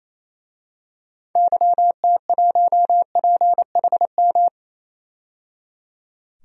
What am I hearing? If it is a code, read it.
Morse code, heard as a keyed tone, about 28 wpm, 700 Hz. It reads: YT1PHM
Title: DXCC-100 Call Signs - 6 Characters - 28wpm